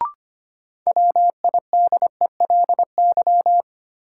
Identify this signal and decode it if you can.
Morse code, heard as a keyed tone, about 25 wpm, 700 Hz. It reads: WIDELY
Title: Sentences from Top 1200 Words - New Word Drill Alt - 25wpm